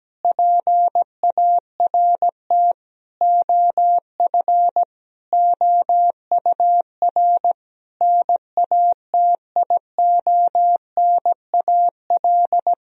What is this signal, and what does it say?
Morse code, heard as a keyed tone, about 17 wpm, 700 Hz. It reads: PART OF OUR NATIONAL
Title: Sentences from Top 1000 Words - Review Part 4 - 17wpm